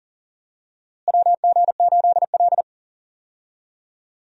Morse code, heard as a keyed tone, about 40 wpm, 700 Hz. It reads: WG8L
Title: USA Call Signs - 4 Characters - Repeated 3x - 40wpm